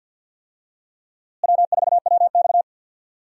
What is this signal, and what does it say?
Morse code, heard as a keyed tone, about 52 wpm, 700 Hz. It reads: W4WX
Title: USA Call Signs - 4 Characters - Repeated 3x - 50wpm